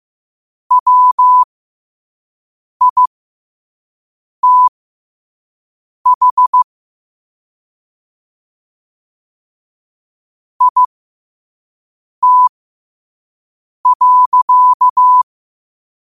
Morse code, heard as a keyed tone, about 15 wpm, spaced as for 5 wpm, 1000 Hz. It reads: WITH IT.
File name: qotdfiles / 5wpm